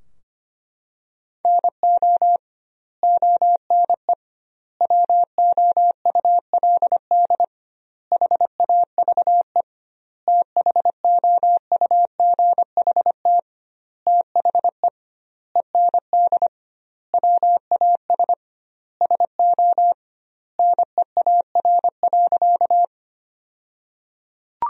Morse code, heard as a keyed tone, about 25 wpm, 700 Hz. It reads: NO ONE WOULD HAVE THOUGHT THE END WAS SO NEAR.
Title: 2x Word-Spacing - Sentences from Top 300 Words - 25wpm